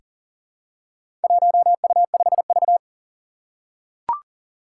Morse code, heard as a keyed tone, about 43 wpm, 700 Hz. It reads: 1U5V
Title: Four Challenging Letter-Number Combinations 40wpm